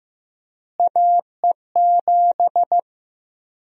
Morse code, heard as a keyed tone, about 15 wpm, 700 Hz. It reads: AE7
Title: USA Call Sign Prefixes - Repeated 2x - 15wpm